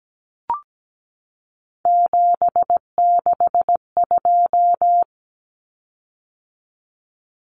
Morse code, heard as a keyed tone, about 17 wpm, 700 Hz. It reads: 762
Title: Common Cut Numbers - 3-Digit 17wpm